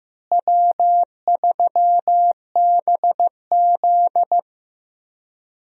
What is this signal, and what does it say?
Morse code, heard as a keyed tone, about 15 wpm, 700 Hz. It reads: W3BZ
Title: USA Call Signs - 4 Characters - Repeated 3x - 15wpm